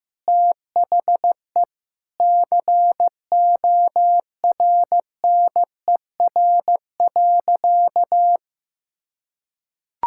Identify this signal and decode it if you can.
Morse code, heard as a keyed tone, about 16 wpm, 700 Hz. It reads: THE CORNER.
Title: Sentences from Top 1600 Words 15wpm